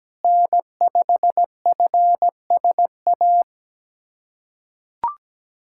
Morse code, heard as a keyed tone, about 17 wpm, 700 Hz. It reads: N5FSA
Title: USA Callsign DE Callsign - 5 Characters - 17wpm